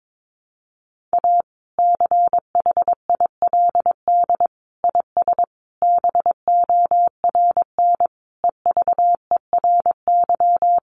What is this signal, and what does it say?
Morse code, heard as a keyed tone, about 22 wpm, 700 Hz. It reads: A CHILD IS BORN EVERY
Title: Sentences from Top 1400 Words 22wpm